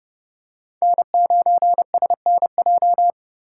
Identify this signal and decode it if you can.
Morse code, heard as a keyed tone, about 30 wpm, 700 Hz. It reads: N9SNJ
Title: USA Call Signs - 5 Characters - 2x Character Spacing - 30wpm